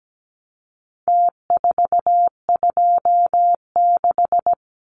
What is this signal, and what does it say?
Morse code, heard as a keyed tone, about 17 wpm, 700 Hz. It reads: T426
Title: Numbers - Contest Style - 4-Digit 17wpm